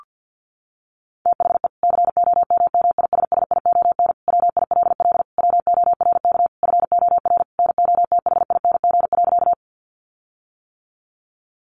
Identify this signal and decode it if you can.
Morse code, heard as a keyed tone, about 50 wpm, 700 Hz. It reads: THE COMMISSION WILL WORK FOR NOTHING.
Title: Sentences from Top 1400 Words 50wpm